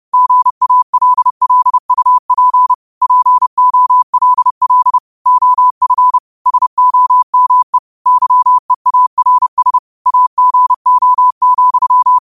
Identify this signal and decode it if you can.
Morse code, heard as a keyed tone, about 30 wpm, 1000 Hz. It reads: GALLUP POLL OF SOME YEARS AGO,
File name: qotdfiles / 30wpm